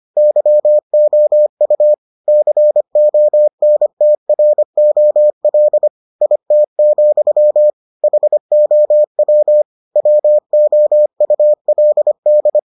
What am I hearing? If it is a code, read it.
Morse code, heard as a keyed tone, about 25 wpm, 600 Hz. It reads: YOU CONTROL IT, HOW WOULD